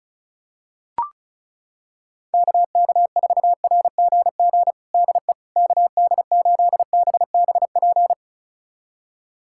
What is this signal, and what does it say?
Morse code, heard as a keyed tone, about 36 wpm, 700 Hz. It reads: KK4RGG DE KD8BBP
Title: USA Callsign DE Callsign 35wpm